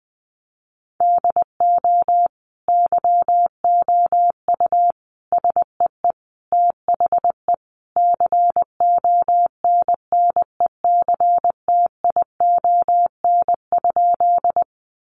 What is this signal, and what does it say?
Morse code, heard as a keyed tone, about 20 wpm, 700 Hz. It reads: DO YOU SEE THE CONNECTION?
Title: Sentences from Top 1300 Words 20wpm